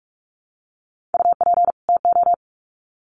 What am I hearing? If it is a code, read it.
Morse code, heard as a keyed tone, about 44 wpm, 700 Hz. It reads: UP TO
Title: Sets of 2 Words 45wpm